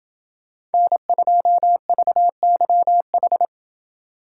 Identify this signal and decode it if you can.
Morse code, heard as a keyed tone, about 30 wpm, 700 Hz. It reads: N2VYH